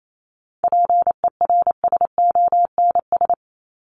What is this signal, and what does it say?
Morse code, heard as a keyed tone, about 28 wpm, 700 Hz. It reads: PERSONS
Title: ES:S Suffix - Top 100 Words with Base Words 28wpm